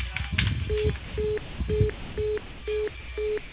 Congestion tone (425 Hz), with noise.